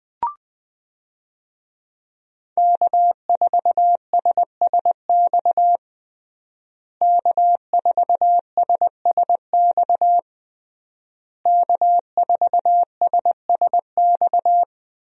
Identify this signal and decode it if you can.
Morse code, heard as a keyed tone, about 21 wpm, 700 Hz. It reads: K4SSX K4SSX K4SSX
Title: USA Call Signs - 5 Characters - Repeated 3x - 20wpm